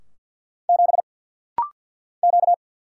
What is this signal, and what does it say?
Morse code, heard as a keyed tone, about 50 wpm, 700 Hz. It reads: 7 Q
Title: Single Letter-Number Alt - Rapid-Fire 50wpm